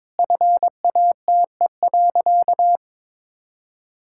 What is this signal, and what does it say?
Morse code, heard as a keyed tone, about 22 wpm, 700 Hz. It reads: FATE.
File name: practice / 22wpm / Sentences with 4-Word Phrases - 22wpm